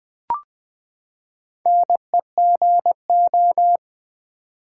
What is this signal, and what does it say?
Morse code, heard as a keyed tone, about 20 wpm, 700 Hz. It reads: NEGO
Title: Build A Word - 4 Letters At A Time - 11-Letter Words - 20wpm